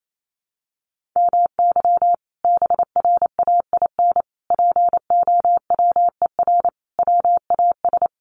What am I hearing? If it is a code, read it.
Morse code, heard as a keyed tone, about 28 wpm, 700 Hz. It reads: MY BRAIN POWER WAS
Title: Sentences from Top 1400 Words 28wpm